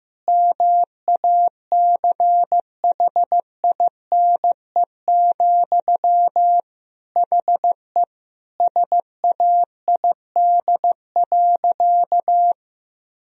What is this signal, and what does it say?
Morse code, heard as a keyed tone, about 16 wpm, 700 Hz. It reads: MACHINE, HE SAID.
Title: Sentences from Top 1000 Words - Review Part 2 - 15wpm